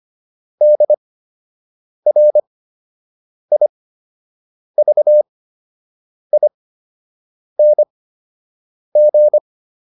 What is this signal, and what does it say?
Morse code, heard as a keyed tone, about 25 wpm, spaced as for 7 wpm, 600 Hz. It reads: DRIVING